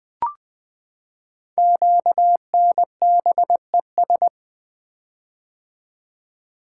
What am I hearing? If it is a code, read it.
Morse code, heard as a keyed tone, about 20 wpm, 700 Hz. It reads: QNBES
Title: Code Groups - 5 Letters - 20wpm